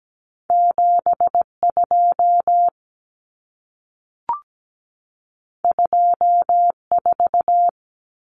Morse code, heard as a keyed tone, about 17 wpm, 700 Hz. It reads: 72 24
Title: Common Cut Numbers - 2-Digit 17wpm